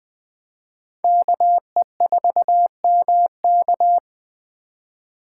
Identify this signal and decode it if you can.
Morse code, heard as a keyed tone, about 20 wpm, 700 Hz. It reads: KE4MK
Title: Speed Racing - USA 2x2 Call Signs 17wpm